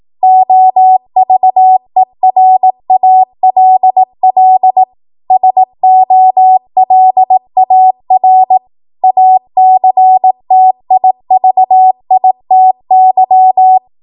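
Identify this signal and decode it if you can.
Morse code, heard as a keyed tone, about 18 wpm, 750 Hz. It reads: OVERALL SOLAR ACTIVITY
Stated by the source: ARRL 18 wpm Morse code practice archive transmission.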